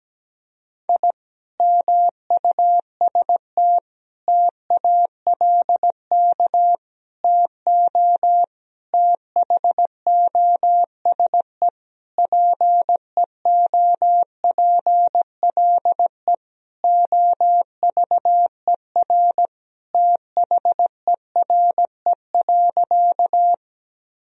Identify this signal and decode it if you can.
Morse code, heard as a keyed tone, about 17 wpm, 700 Hz. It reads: I MUST TALK TO THOSE PEOPLE OVER THERE.
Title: Sentences from Top 1000 Words - Review Part 1 - 17wpm